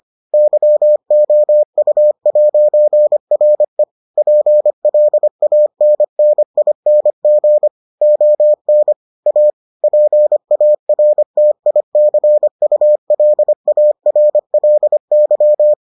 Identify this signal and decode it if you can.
Morse code, heard as a keyed tone, about 25 wpm, 600 Hz. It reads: YOU'RE PLANNING ON A PARTICULARLY